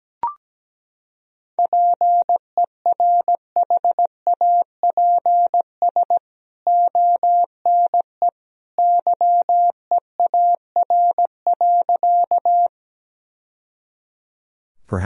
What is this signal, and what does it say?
Morse code, heard as a keyed tone, about 17 wpm, 700 Hz. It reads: PERHAPS ONE YEAR.
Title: Sentences from Top 1000 Words - Review Part 1 - 17wpm